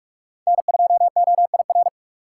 Morse code, heard as a keyed tone, about 46 wpm, 700 Hz. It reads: N2OIR
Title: USA Call Signs - 5 Characters - 3x Character Spacing - 45wpm